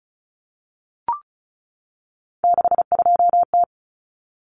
Morse code, heard as a keyed tone, about 34 wpm, 700 Hz. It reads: 62T
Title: Common Cut Numbers - 3-Digit 35wpm